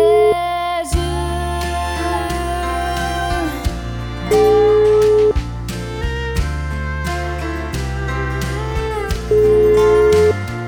Ringback tone (Europe, 425 Hz), with noise.